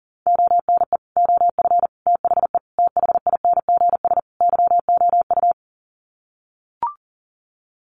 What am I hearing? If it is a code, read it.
Morse code, heard as a keyed tone, about 40 wpm, 700 Hz. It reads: ONE OF THE THINGS YOU
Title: Sets of 5 Words Encore2 40wpm